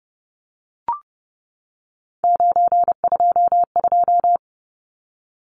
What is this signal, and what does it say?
Morse code, heard as a keyed tone, about 30 wpm, 700 Hz. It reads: 922
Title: Numbers - 3-Digit 30wpm